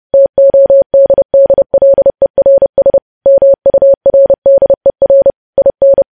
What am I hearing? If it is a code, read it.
Morse code, heard as a keyed tone, about 30 wpm, 550 Hz. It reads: TODDLERS MURDER IN
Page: news headlines in morse code at 30 wpm